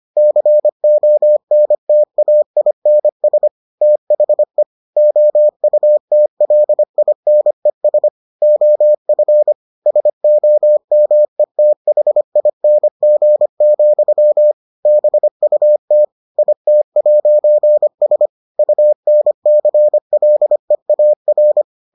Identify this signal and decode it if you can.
Morse code, heard as a keyed tone, about 25 wpm, 600 Hz. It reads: CONTAINS THE OUTLINES OF SOMETHING, BUT IT'S UNCLEAR